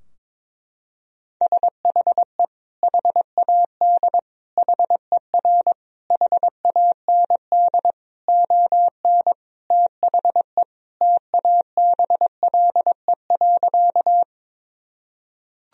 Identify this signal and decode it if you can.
Morse code, heard as a keyed tone, about 22 wpm, 700 Hz. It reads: SHE HAD HER HAND ON THE TABLE.
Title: Sentences from Top 1000 Words - Review Part 2 - 22wpm